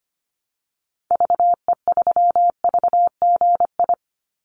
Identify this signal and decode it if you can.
Morse code, heard as a keyed tone, about 25 wpm, 700 Hz. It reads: VE3VGI
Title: Call Signs Repeated 3x - 25wpm